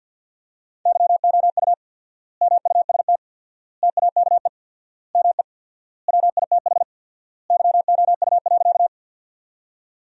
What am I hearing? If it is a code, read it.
Morse code, heard as a keyed tone, about 49 wpm, 700 Hz. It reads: YOU MUST TAKE ME WITH YOU.
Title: 4x Word-Spacing - Sentences from Top 200 Words - 50wpm